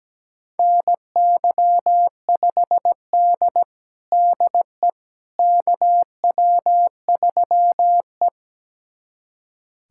Morse code, heard as a keyed tone, about 17 wpm, 700 Hz. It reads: NY5D DE KW3E
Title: USA Callsign DE Callsign - 4 Characters - 17wpm